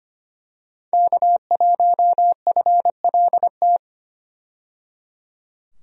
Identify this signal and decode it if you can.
Morse code, heard as a keyed tone, about 25 wpm, 700 Hz. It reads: K1FLT
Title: USA Call Signs - 5 Characters - 25wpm